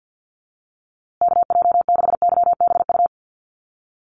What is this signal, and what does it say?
Morse code, heard as a keyed tone, about 50 wpm, 700 Hz. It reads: KJ6YBU
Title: USA Call Signs Repeated 2x 50wpm